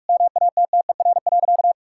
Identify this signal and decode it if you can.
Morse code, heard as a keyed tone, about 41 wpm, 700 Hz. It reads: MATTER.
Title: Sentences from Top 1000 Words - Review Part 5 - 45wpm